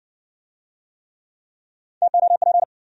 Morse code, heard as a keyed tone, about 60 wpm, 700 Hz. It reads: TOP